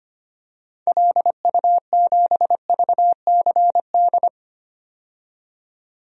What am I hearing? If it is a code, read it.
Morse code, heard as a keyed tone, about 25 wpm, 700 Hz. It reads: LU7VCD
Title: DXCC-100 Call Signs - 6 Characters - 25wpm